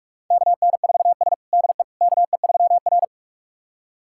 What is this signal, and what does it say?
Morse code, heard as a keyed tone, about 46 wpm, 700 Hz. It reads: KN4S DE KE3R
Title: USA Callsign DE Callsign - 4 Characters - 45wpm